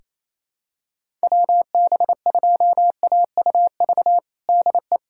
Morse code, heard as a keyed tone, about 28 wpm, 700 Hz. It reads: WB2AUV DE
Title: USA Callsign DE Callsign - 6 Characters - 28wpm